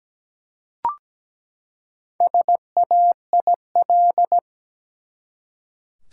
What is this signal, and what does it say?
Morse code, heard as a keyed tone, about 17 wpm, 700 Hz. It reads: SAIL
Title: Four Letter Words - 17wpm